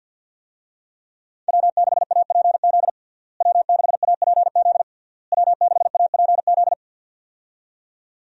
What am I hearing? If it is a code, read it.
Morse code, heard as a keyed tone, about 50 wpm, 700 Hz. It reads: W6APZ W6APZ W6APZ
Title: USA Call Signs - 5 Characters - Repeated 3x - 50wpm